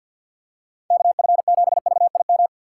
Morse code, heard as a keyed tone, about 52 wpm, 700 Hz. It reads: KF7VIM
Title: USA Call Signs - 6 Characters - 4x Character Spacing - 50wpm